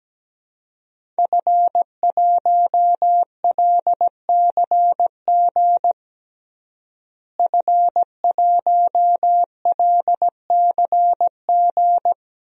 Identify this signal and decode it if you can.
Morse code, heard as a keyed tone, about 17 wpm, 700 Hz. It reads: F1LCG F1LCG
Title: DXCC-100 Call Signs - 5 Characters - Repeated 3x - 17wpm